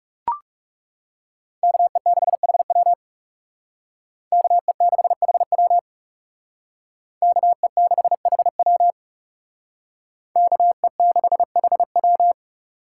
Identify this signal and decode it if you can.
Morse code, heard as a keyed tone, about 34 wpm, 700 Hz. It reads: KE6HW KE6HW KE6HW KE6HW
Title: Speed Racing - USA 2x2 Call Signs 30wpm